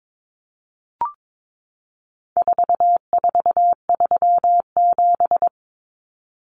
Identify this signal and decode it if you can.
Morse code, heard as a keyed tone, about 22 wpm, 700 Hz. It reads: 4437
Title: Numbers - 4-Digit 22wpm